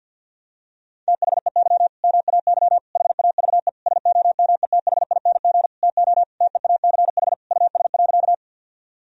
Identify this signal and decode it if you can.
Morse code, heard as a keyed tone, about 52 wpm, 700 Hz. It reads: THEY MAY HAVE SOMETHING TO TEACH US.
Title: Sentences from Top 1000 Words - Review Part 2 - 50wpm